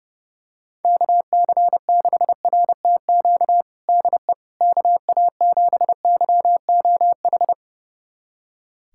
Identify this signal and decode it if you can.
Morse code, heard as a keyed tone, about 29 wpm, 700 Hz. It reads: KC6RTQ DE KA7YOH
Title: USA Callsign DE Callsign - 6 Characters - 30wpm